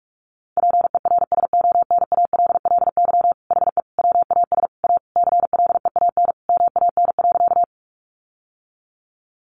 Morse code, heard as a keyed tone, about 44 wpm, 700 Hz. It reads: PERSONALLY HE WAS A CLEAN MAN.
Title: Sentences from Top 1400 Words 45wpm